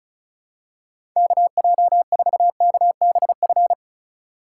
Morse code, heard as a keyed tone, about 36 wpm, 700 Hz. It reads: KJ4KBF
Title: USA Call Signs - 6 Characters - 3x Character Spacing - 35wpm